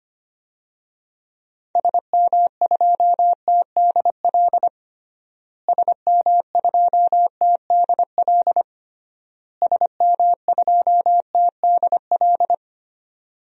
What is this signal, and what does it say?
Morse code, heard as a keyed tone, about 25 wpm, 700 Hz. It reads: SM2TDL SM2TDL SM2TDL
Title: DXCC-100 Call Signs - 6 Characters - Repeated 3x - 25wpm